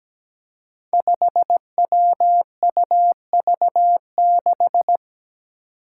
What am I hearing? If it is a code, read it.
Morse code, heard as a keyed tone, about 17 wpm, 700 Hz. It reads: 5WUV6
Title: Five Challenging Letter-Number Combinations 17wpm